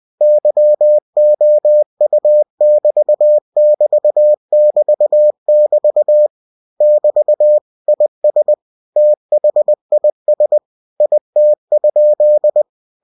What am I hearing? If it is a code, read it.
Morse code, heard as a keyed tone, about 20 wpm, 600 Hz. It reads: YOU<BT><BT><BT><BT> <BT> IS THIS IT?
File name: cw-books / winnie-the-pooh / winnie-20wpm-page-0051